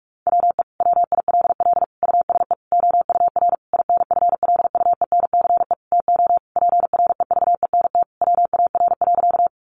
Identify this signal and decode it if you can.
Morse code, heard as a keyed tone, about 44 wpm, 700 Hz. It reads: WE WILL USE OUR INFLUENCE TO PREVENT WAR.
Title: Sentences from Top 1100 Words 45wpm